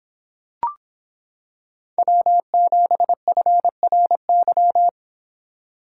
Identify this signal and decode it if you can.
Morse code, heard as a keyed tone, about 26 wpm, 700 Hz. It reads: W7FRY